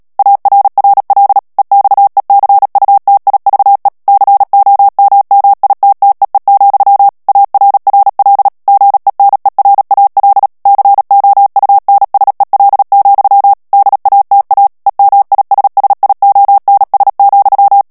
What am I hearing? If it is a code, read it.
Morse code, heard as a keyed tone, about 37 wpm, 800 Hz. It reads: ARRL EXECUTIVE COMMITTEE, ARRL GENERAL COUNSEL, DATA EMISSIONS,
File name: morse mp3 / ANGLAIS texte clair 3 vitesse 35 mots minute